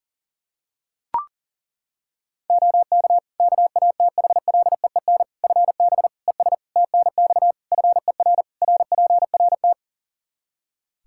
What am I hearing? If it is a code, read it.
Morse code, heard as a keyed tone, about 40 wpm, 700 Hz. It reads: OK KATHLEEN FB ES TNX FER RPRT
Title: Practice QSO - Exchange 2 - Info - 40wpm